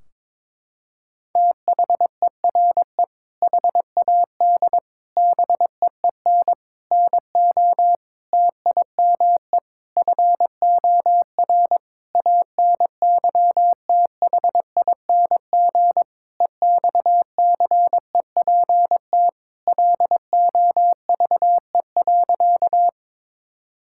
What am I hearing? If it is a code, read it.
Morse code, heard as a keyed tone, about 22 wpm, 700 Hz. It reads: THERE HAD BEEN NO TIME FOR ANYTHING EXCEPT LOVE.
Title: Sentences from Top 1600 Words 22wpm